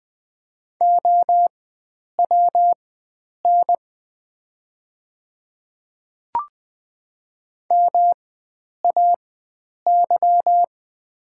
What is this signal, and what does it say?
Morse code, heard as a keyed tone, about 20 wpm, spaced as for 9 wpm, 700 Hz. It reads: OWN MAY
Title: 4x Character Spacing - Top 300 Words - 20wpm